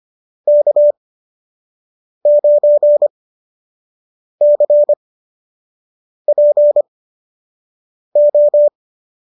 Morse code, heard as a keyed tone, about 25 wpm, spaced as for 6 wpm, 590 Hz. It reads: K9CPO